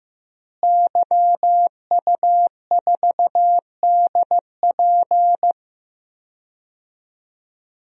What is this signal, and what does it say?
Morse code, heard as a keyed tone, about 15 wpm, 700 Hz. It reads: YU4DP
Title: DXCC-100 Call Signs - 5 Characters - Repeated 2x - 15wpm